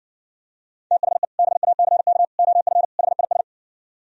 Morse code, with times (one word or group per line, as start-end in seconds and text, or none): 0.9-1.2 THE
1.4-2.2 BACK
2.4-2.8 OF
3.0-3.4 HIS